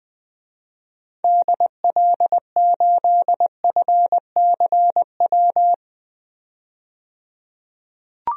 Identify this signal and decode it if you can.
Morse code, heard as a keyed tone, about 20 wpm, 700 Hz. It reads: DL8FCW
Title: DXCC-100 Call Signs - 6 Characters - 20wpm